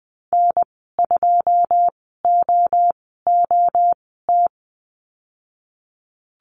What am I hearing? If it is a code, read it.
Morse code, heard as a keyed tone, about 19 wpm, spaced as for 14 wpm, 700 Hz. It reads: N2OOT
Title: 088 Lesson 15g - Number 5 - Call Signs 20wpm